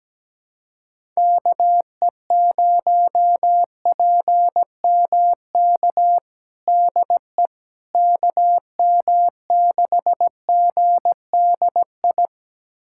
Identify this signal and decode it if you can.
Morse code, heard as a keyed tone, about 17 wpm, 700 Hz. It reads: KE0PMK DE KM6GDI
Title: USA Callsign DE Callsign - 6 Characters - 17wpm